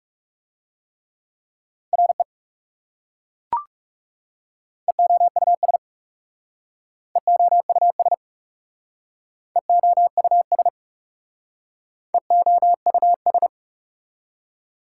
Morse code, heard as a keyed tone, about 38 wpm, 690 Hz. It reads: RE EOUS EOUS EOUS EOUS